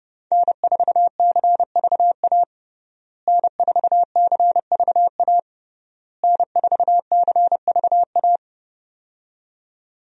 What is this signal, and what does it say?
Morse code, heard as a keyed tone, about 29 wpm, 700 Hz. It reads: N4CVA N4CVA N4CVA
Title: USA Call Signs - 5 Characters - Repeated 3x - 30wpm